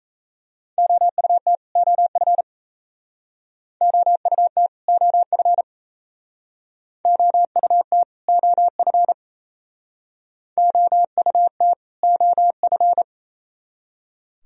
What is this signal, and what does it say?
Morse code, heard as a keyed tone, about 36 wpm, 690 Hz. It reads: OUT OF OUT OF OUT OF OUT OF